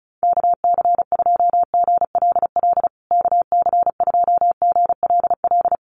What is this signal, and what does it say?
Morse code, heard as a keyed tone, about 34 wpm, 700 Hz. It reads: KC2GLL KC2GLL
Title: USA Call Signs Repeated 3x 35wpm